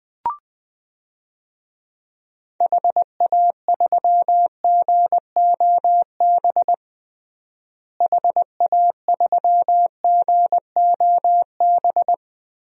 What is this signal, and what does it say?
Morse code, heard as a keyed tone, about 20 wpm, 700 Hz. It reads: HA3GOB HA3GOB
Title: DXCC-100 Call Signs - 6 Characters - Repeated 3x - 20wpm